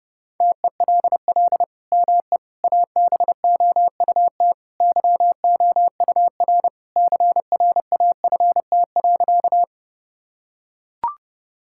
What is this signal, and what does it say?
Morse code, heard as a keyed tone, about 30 wpm, 700 Hz. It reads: TELL ME ABOUT YOUR CRAFT.
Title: Sentences from Top 1000 Words - Review Part 6 - 30wpm